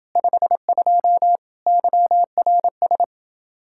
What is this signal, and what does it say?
Morse code, heard as a keyed tone, about 29 wpm, 700 Hz. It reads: 52 YRS